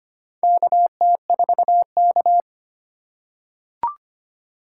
Morse code, heard as a keyed tone, about 25 wpm, 700 Hz. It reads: KT4K
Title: USA Call Signs - 4 Characters - 4x Character Spacing - 25wpm